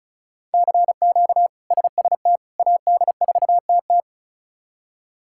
Morse code, heard as a keyed tone, about 36 wpm, 700 Hz. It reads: CQ SST AD4TT
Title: K1USN SST QSO Practice - 35wpm